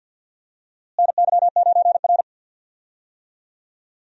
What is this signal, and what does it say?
Morse code, heard as a keyed tone, about 51 wpm, 700 Hz. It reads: NY9R
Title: USA Call Signs - 4 Characters - Repeated 3x - 50wpm